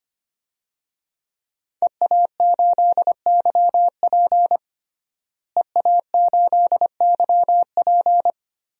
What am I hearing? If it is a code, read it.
Morse code, heard as a keyed tone, about 25 wpm, 700 Hz. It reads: EA8YP EA8YP
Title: DXCC-100 Call Signs - 5 Characters - Repeated 3x - 25wpm